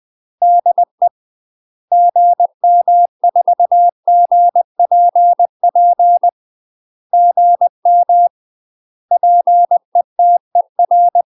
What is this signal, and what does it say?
Morse code, heard as a keyed tone, about 20 wpm, 700 Hz. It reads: DE GM4GPP GM PETER
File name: UK MASTER QSOs TRAINING / 20WPM / UK MASTER QSOs 0011